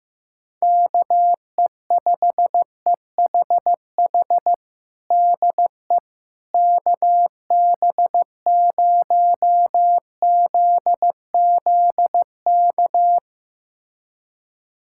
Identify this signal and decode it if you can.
Morse code, heard as a keyed tone, about 15 wpm, 700 Hz. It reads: KE5EHH DE KB0ZZK
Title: USA Callsign DE Callsign - 6 Characters - 15wpm